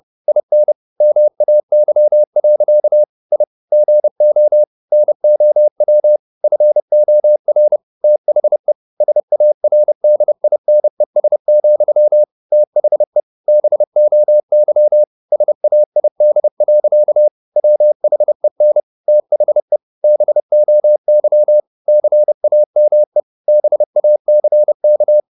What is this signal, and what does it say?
Morse code, heard as a keyed tone, about 30 wpm, 600 Hz. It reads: IN MAY. I GO NOW FOR THE SARDINES, THE BOY SAID. WHEN THE BOY CAME BACK
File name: the old man and the sea / the old man and the sea 1 30wpm